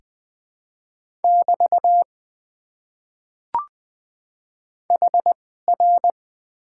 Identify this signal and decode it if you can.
Morse code, heard as a keyed tone, about 20 wpm, spaced as for 15 wpm, 700 Hz. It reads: <BT> HR
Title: Top 90 QSO Elements - 2x Character Spacing - 20wpm